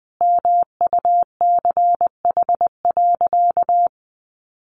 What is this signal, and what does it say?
Morse code, heard as a keyed tone, about 20 wpm, 700 Hz. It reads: MUCH.
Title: Sentences from Top 1200 Words 20wpm